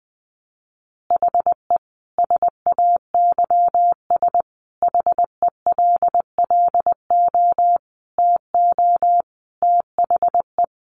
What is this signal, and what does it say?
Morse code, heard as a keyed tone, about 20 wpm, 700 Hz. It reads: HE SAYS HELLO TO THE
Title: Easy Phrases 20wpm